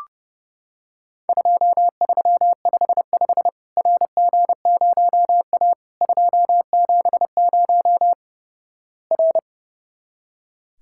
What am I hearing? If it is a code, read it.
Morse code, heard as a keyed tone, about 30 wpm, 700 Hz. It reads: 2355 RG0A 270 R
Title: Worked All Europe DX Contest - QTC Exchange - 30wpm